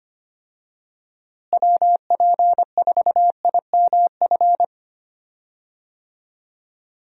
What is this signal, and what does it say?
Morse code, heard as a keyed tone, about 25 wpm, 700 Hz. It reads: WP4IMF
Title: USA Call Signs - 6 Characters - Repeated 2x - 25wpm